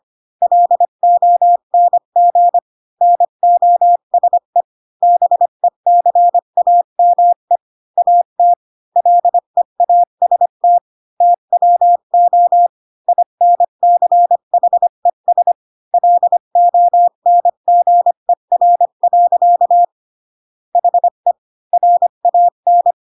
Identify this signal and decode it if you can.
Morse code, heard as a keyed tone, about 25 wpm, 700 Hz. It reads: LONG NOSE BECAME AT LEAST TWO INCHES LONGER. HE RAN
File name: Pinocchio / Pinocchio ch5 20 25 ik1hge